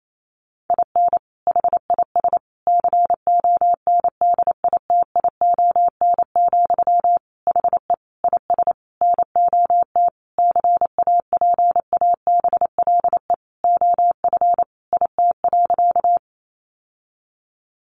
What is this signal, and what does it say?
Morse code, heard as a keyed tone, about 28 wpm, 700 Hz. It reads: IN HIS CONDITION, HE IS NOT CAPABLE OF IT.
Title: Sentences from Top 1200 Words 28wpm